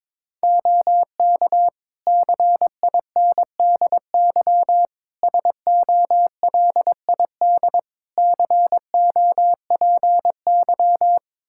Morse code, heard as a keyed tone, about 22 wpm, 700 Hz. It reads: OK CINDY SOLID COPY
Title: K1USN SST QSO Practice 22wpm